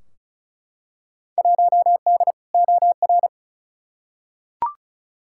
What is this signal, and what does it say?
Morse code, heard as a keyed tone, about 36 wpm, 700 Hz. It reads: 1D OR
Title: Field Day Exchanges - 35wpm